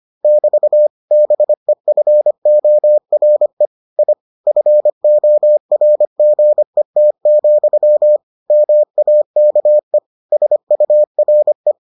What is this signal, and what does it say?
Morse code, heard as a keyed tone, about 25 wpm, 600 Hz. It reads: <BT> BEFORE I FORGET, MAKE SURE